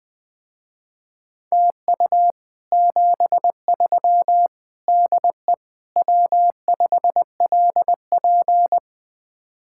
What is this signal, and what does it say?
Morse code, heard as a keyed tone, about 20 wpm, 700 Hz. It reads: TU 73 DE W5LP